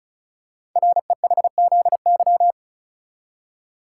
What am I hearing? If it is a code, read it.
Morse code, heard as a keyed tone, about 36 wpm, 700 Hz. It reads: REHZY
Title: Code Groups - 5 Letters - 35wpm